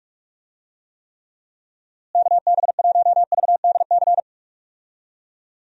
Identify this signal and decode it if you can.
Morse code, heard as a keyed tone, about 46 wpm, 700 Hz. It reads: KB1VDC
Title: USA Call Signs - 6 Characters - Repeated 3x - 45wpm